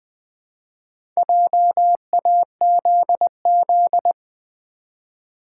Morse code, heard as a keyed tone, about 20 wpm, 700 Hz. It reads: JAZZ
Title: Build A Word - 2 Letters At A Time - 4-Letter Words - 20wpm